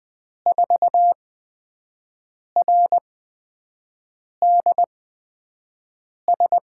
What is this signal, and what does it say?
Morse code, heard as a keyed tone, about 20 wpm, spaced as for 5 wpm, 700 Hz. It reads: 4RDS
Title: USA Call Signs - 6 Characters - 8x Character Spacing - 20wpm